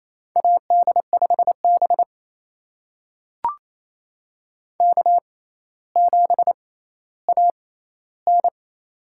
Morse code, heard as a keyed tone, about 28 wpm, spaced as for 14 wpm, 700 Hz. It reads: AD5B K7AN
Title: USA Call Signs - 4 Characters - 6x Character Spacing - 28wpm